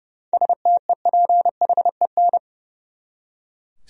Morse code, heard as a keyed tone, about 31 wpm, 700 Hz. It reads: STEPHEN
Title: USA Common Names - 30wpm